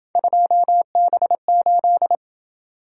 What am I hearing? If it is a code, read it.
Morse code, heard as a keyed tone, about 28 wpm, 700 Hz. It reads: 2B8